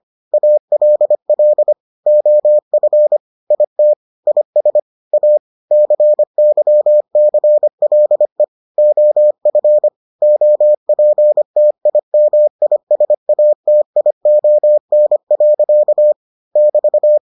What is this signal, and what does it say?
Morse code, heard as a keyed tone, about 25 wpm, 600 Hz. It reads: ALL OF IT IS A CYCLE OF OPTIMISATION. <BT>